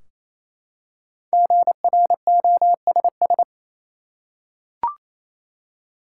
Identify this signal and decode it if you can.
Morse code, heard as a keyed tone, about 28 wpm, 700 Hz. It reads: GROSS
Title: Five Letter Words - 28wpm